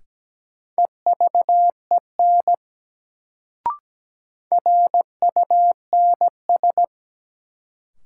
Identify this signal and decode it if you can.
Morse code, heard as a keyed tone, about 17 wpm, 700 Hz. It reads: EVEN RUNS
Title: Sentences from Top 600 Words - New Word Drill - 17wpm